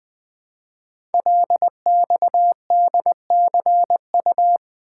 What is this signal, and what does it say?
Morse code, heard as a keyed tone, about 20 wpm, 700 Hz. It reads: LXDCU
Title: Code Groups - 5 Letters - 20wpm